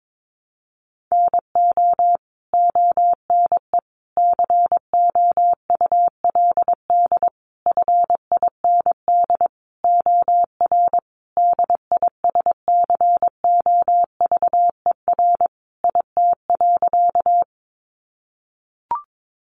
Morse code, heard as a keyed tone, about 22 wpm, 700 Hz. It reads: NO ONE COULD FIND OR DISCOVER IT.
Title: Sentences from Top 1100 Words 22wpm